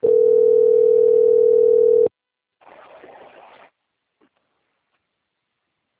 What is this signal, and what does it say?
Ringback tone (North America), 440 + 480 Hz.